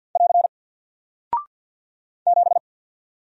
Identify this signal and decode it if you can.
Morse code, heard as a keyed tone, about 48 wpm, 700 Hz. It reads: <AR> 7
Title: Single Character (Letter:Number:Punctuation:Prosign) Alt 50wpm